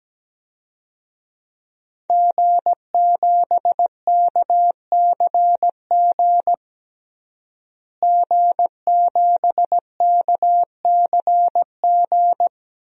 Morse code, heard as a keyed tone, about 17 wpm, 700 Hz. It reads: G7KCG G7KCG
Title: DXCC-100 Call Signs - 5 Characters - Repeated 3x - 17wpm